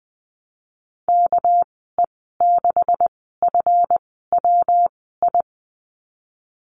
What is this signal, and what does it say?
Morse code, heard as a keyed tone, about 20 wpm, spaced as for 15 wpm, 700 Hz. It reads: KE6FWI
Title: 200 Lesson 32g - Letter k - Call Signs 20wpm